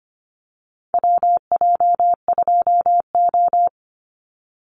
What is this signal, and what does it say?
Morse code, heard as a keyed tone, about 25 wpm, 700 Hz. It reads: WJ2O
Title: Call Signs - 25wpm